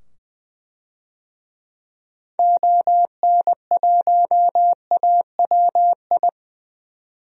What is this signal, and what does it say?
Morse code, heard as a keyed tone, about 20 wpm, 700 Hz. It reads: ON1AWI
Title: DXCC-100 Call Signs - 6 Characters - 20wpm